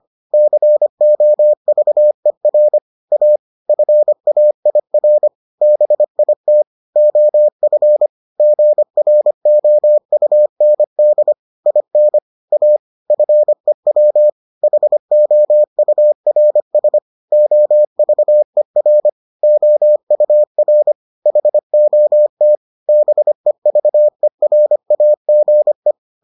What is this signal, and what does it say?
Morse code, heard as a keyed tone, about 25 wpm, 600 Hz. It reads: COVER A FAIR BIT OF GROUND IN A FEW HOURS OVER OUR HOT BEVERAGE